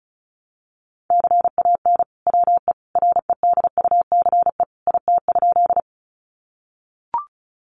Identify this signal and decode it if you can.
Morse code, heard as a keyed tone, about 34 wpm, 700 Hz. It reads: CAN WE REDUCE IT?
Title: Sentences from Top 1100 Words 35wpm